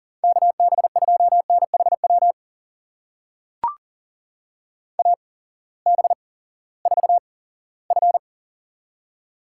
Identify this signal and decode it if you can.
Morse code, heard as a keyed tone, about 42 wpm, spaced as for 20 wpm, 700 Hz. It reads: KB2NHW AB4F
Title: USA Call Signs - 8x Character Spacing - 40wpm